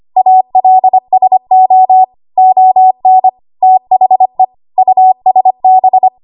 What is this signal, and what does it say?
Morse code, heard as a keyed tone, about 25 wpm, 750 Hz. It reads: ALSO ON THE USB
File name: Archive / 25 WPM / 25WPM